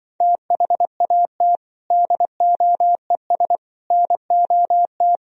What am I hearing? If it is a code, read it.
Morse code, heard as a keyed tone, about 24 wpm, 700 Hz. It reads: THAT DOES NOT
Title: Speed-Racing - Sets of 3 Words 20wpm